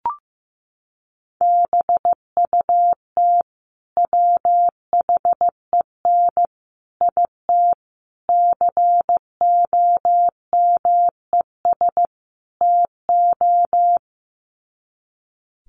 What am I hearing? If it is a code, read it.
Morse code, heard as a keyed tone, about 15 wpm, 700 Hz. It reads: BUT WHEN IT COMES TO
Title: Sets of 5 Words 15wpm